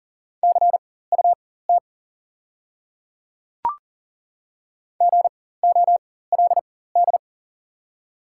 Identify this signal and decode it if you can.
Morse code, heard as a keyed tone, about 40 wpm, spaced as for 19 wpm, 700 Hz. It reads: CUT GOLD
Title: 4x Character Spacing - Top 500 Words - 40wpm